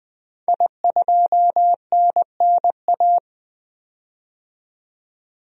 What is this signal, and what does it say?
Morse code, heard as a keyed tone, about 20 wpm, 700 Hz. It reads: I2NNA
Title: DXCC-100 Call Signs - 5 Characters - 20wpm